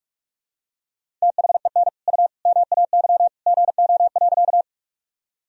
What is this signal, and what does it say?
Morse code, heard as a keyed tone, about 46 wpm, 700 Hz. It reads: THEN U MAY GO.
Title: Sentences from Top 100 Words and Abbreviations 45wpm